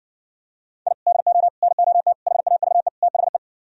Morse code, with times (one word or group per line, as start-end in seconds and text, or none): 0.9-0.9 I
1.1-1.5 DO
1.6-2.1 NOT
2.3-2.9 HAVE
3.0-3.4 THE